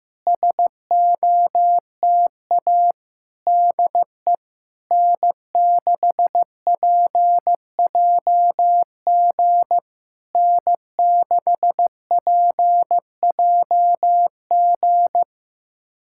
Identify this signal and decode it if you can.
Morse code, heard as a keyed tone, about 15 wpm, 700 Hz. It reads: SOTA DE N6PJG N6PJG
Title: SOTA QSO Practice 15wpm